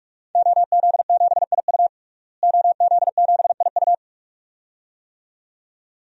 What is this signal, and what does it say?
Morse code, heard as a keyed tone, about 41 wpm, 700 Hz. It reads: OZ7IU OZ7IU
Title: DXCC-100 Call Signs - 5 Characters - Repeated 2x - 45wpm